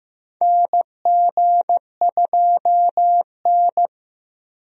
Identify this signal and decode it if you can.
Morse code, heard as a keyed tone, about 15 wpm, 700 Hz. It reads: NG2N
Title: USA Call Signs - 4 Characters - Repeated 2x - 15wpm